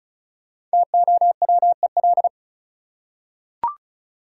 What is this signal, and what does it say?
Morse code, heard as a keyed tone, about 34 wpm, 700 Hz. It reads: TOWEL
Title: Five Letter Words - 35wpm